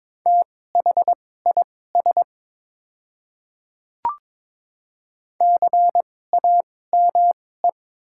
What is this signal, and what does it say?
Morse code, heard as a keyed tone, about 22 wpm, spaced as for 16 wpm, 700 Hz. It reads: THIS CAME
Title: 2x Character Spacing - Top 300 Words - 22wpm